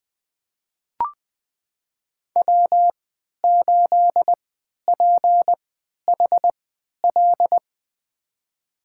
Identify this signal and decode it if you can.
Morse code, heard as a keyed tone, about 20 wpm, spaced as for 11 wpm, 700 Hz. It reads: W8PHL
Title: USA Call Signs - 5 Characters - 3x Character Spacing - 20wpm